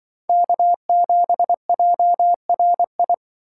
Morse code, heard as a keyed tone, about 25 wpm, 700 Hz. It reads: K7JRI
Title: Speed Racing - USA 1x3 Call Signs 20wpm